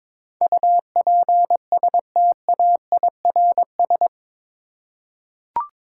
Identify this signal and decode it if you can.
Morse code, heard as a keyed tone, about 22 wpm, 700 Hz. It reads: UPSTAIRS
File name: practice / 22wpm / Eight Letter Words - 22wpm